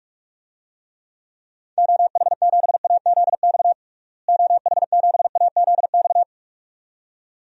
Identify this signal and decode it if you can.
Morse code, heard as a keyed tone, about 46 wpm, 700 Hz. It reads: OH7AZX OH7AZX
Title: DXCC-100 Call Signs - 6 Characters - Repeated 2x - 45wpm